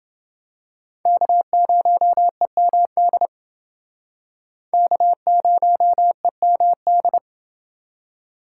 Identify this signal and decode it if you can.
Morse code, heard as a keyed tone, about 28 wpm, 700 Hz. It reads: K0EMD K0EMD